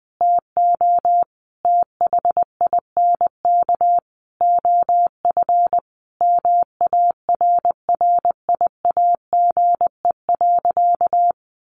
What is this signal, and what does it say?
Morse code, heard as a keyed tone, about 20 wpm, 700 Hz. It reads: TO THINK OF MARRIAGE.
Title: Sentences from Top 800 Words 20wpm